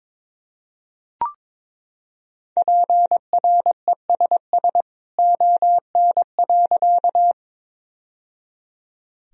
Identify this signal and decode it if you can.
Morse code, heard as a keyed tone, about 22 wpm, 700 Hz. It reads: PRESS ON.
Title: Sentences from Top 1000 Words - Review Part 1 - 22wpm